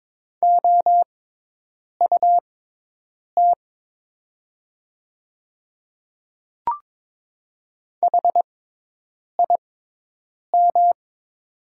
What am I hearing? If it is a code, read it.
Morse code, heard as a keyed tone, about 22 wpm, spaced as for 8 wpm, 700 Hz. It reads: OUT HIM
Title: 6x Character Spacing - Top 100 Words - 22wpm